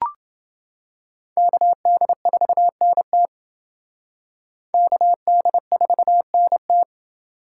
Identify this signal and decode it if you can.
Morse code, heard as a keyed tone, about 30 wpm, 700 Hz. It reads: KD4NT KD4NT